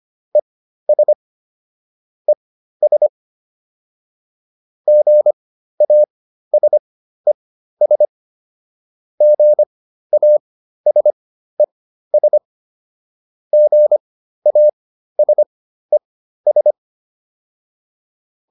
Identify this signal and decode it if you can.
Morse code, heard as a keyed tone, about 25 wpm, spaced as for 13 wpm, 600 Hz. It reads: ES ES GASES GASES GASES